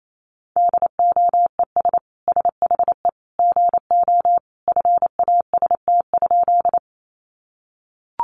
Morse code, heard as a keyed tone, about 28 wpm, 700 Hz. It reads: DOES SHE GO FAST?
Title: Sentences from Top 400 Words 28wpm